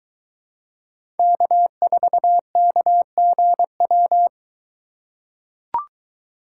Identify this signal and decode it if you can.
Morse code, heard as a keyed tone, about 23 wpm, 700 Hz. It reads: K4KGW